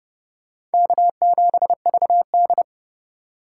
Morse code, heard as a keyed tone, about 32 wpm, 700 Hz. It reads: K7VD
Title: USA Call Signs - 4 Characters - 4x Character Spacing - 30wpm